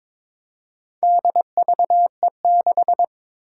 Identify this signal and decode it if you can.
Morse code, heard as a keyed tone, about 23 wpm, 700 Hz. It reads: DVE6
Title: Four Challenging Letter-Number Combinations 22wpm